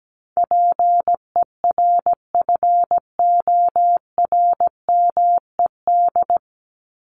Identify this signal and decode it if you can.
Morse code, heard as a keyed tone, about 17 wpm, 700 Hz. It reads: PERFORMED
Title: ED Suffix - Top 100 Words with Base Words 17wpm